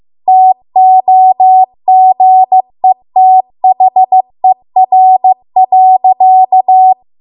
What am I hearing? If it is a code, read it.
Morse code, heard as a keyed tone, about 15 wpm, 750 Hz. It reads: TOGETHER.
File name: Archive / 15 WPM / 15WPM